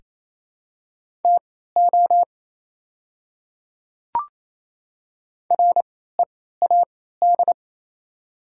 Morse code, heard as a keyed tone, about 28 wpm, spaced as for 16 wpm, 700 Hz. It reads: TO READ
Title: 3x Character Spacing - Top 300 Words - 28wpm